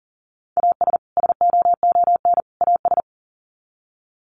Morse code, heard as a keyed tone, about 40 wpm, 700 Hz. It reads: AS SOON AS
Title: Sets of 3 Words 40wpm